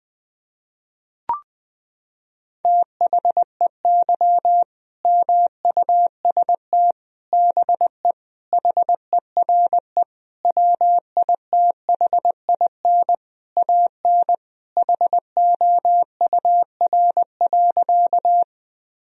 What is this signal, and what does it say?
Morse code, heard as a keyed tone, about 20 wpm, 700 Hz. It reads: THEY MUST BE HERE WITHIN AN HOUR.
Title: Sentences from Top 1000 Words - Review Part 4 - 20wpm